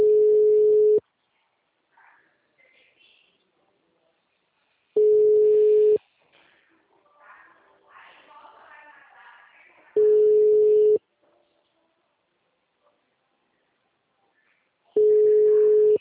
Ringback tone (Europe, 425 Hz).